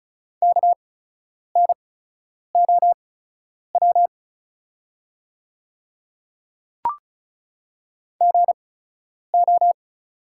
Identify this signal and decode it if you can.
Morse code, heard as a keyed tone, about 36 wpm, spaced as for 10 wpm, 700 Hz. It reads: KNOW GO
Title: 8x Character Spacing - Top 300 Words - 35wpm